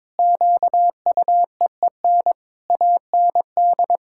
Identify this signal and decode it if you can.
Morse code, heard as a keyed tone, about 22 wpm, 700 Hz. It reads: QUEEN AND
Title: Pangram Challenge v2 - 22wpm